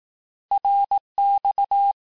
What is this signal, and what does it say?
Morse code, heard as a keyed tone, about 18 wpm, 800 Hz. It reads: RX
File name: cours-du-lundi files / cours-1852-abreviations-vitesse-17MPM-puis-22MPM